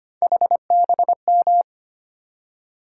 Morse code, heard as a keyed tone, about 25 wpm, 700 Hz. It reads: HBM
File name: Three Letter-Number Combinations - 25wpm